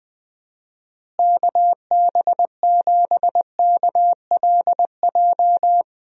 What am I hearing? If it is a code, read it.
Morse code, heard as a keyed tone, about 19 wpm, 700 Hz. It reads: KB7KLJ
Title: USA Call Signs - 6 Characters - Repeated 2x - 20wpm